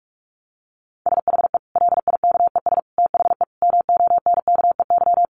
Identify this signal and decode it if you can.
Morse code, heard as a keyed tone, about 44 wpm, 700 Hz. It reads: SHE LIKES THE MONKEY
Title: Easy Phrases 45wpm